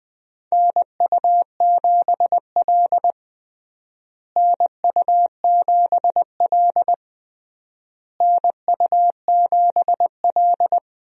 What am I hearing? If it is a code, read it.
Morse code, heard as a keyed tone, about 20 wpm, 700 Hz. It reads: NU7L NU7L NU7L
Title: USA Call Signs - 4 Characters - Repeated 3x - 20wpm